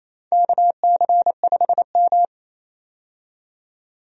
Morse code, heard as a keyed tone, about 28 wpm, 690 Hz. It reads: KC5M
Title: USA Call Signs - 4 Characters - 28wpm